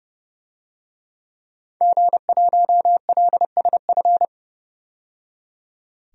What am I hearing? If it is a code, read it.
Morse code, heard as a keyed tone, about 31 wpm, 700 Hz. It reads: G1LSF
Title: DXCC-100 Call Signs - 5 Characters - 30wpm